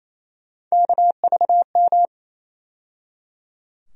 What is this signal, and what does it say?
Morse code, heard as a keyed tone, about 28 wpm, 700 Hz. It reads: KVM
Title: USA Call Sign Postfixes - 3 Letters 28wpm